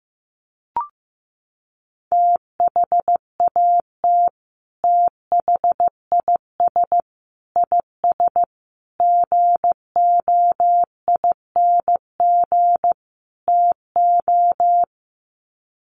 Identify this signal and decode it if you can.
Morse code, heard as a keyed tone, about 15 wpm, 700 Hz. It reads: THAT THIS IS GOING TO
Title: Sets of 5 Words Encore2 15wpm